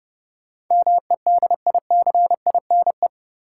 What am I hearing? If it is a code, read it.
Morse code, heard as a keyed tone, about 32 wpm, 700 Hz. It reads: MEDICINE
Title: Sentences from Top 900 Words - New Word Drill - 30wpm